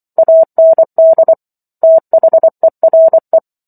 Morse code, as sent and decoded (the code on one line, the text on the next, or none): .- -. -.. / - .... .-. .
AND THERE